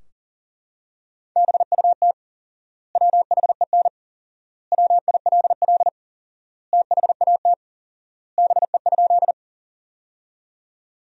Morse code, heard as a keyed tone, about 40 wpm, 700 Hz. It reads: BUT WHEN WILL THAT BE?
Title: 4x Word-Spacing - Sentences from Top 100 Words - 40wpm